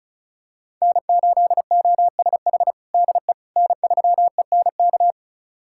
Morse code, heard as a keyed tone, about 34 wpm, 700 Hz. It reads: N8OSH DE N3ENK
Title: USA Callsign DE Callsign - 5 Characters - 35wpm